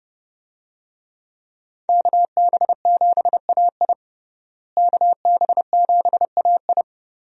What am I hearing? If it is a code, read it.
Morse code, heard as a keyed tone, about 30 wpm, 700 Hz. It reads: KB7AI KB7AI
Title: USA Call Signs - 5 Characters - Repeated 3x - 30wpm